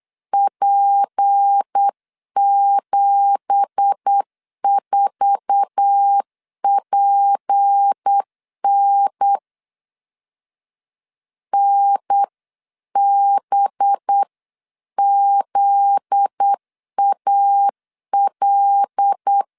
Morse code, heard as a keyed tone, about 8 wpm, 800 Hz. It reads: P74PN NBZAL
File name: CW / DET / 1 9 1500C S C 8wpm 5653